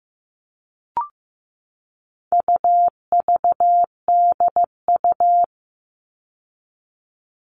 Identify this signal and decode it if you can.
Morse code, heard as a keyed tone, about 15 wpm, 700 Hz. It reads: UVDU
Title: All Cut Numbers - 4-Digit 15wpm